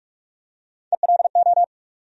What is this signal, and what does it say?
Morse code, heard as a keyed tone, about 42 wpm, 700 Hz. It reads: ELO
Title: USA Call Sign Postfixes - 3 Letters 45wpm